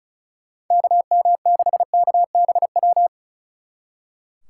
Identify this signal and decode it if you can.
Morse code, heard as a keyed tone, about 44 wpm, 700 Hz. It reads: KM6KBW